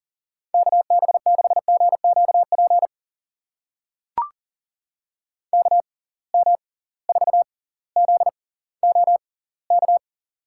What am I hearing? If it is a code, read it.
Morse code, heard as a keyed tone, about 40 wpm, spaced as for 21 wpm, 700 Hz. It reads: KB6GQP KM4ZOK
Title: USA Call Signs - 6 Characters - 6x Character Spacing - 40wpm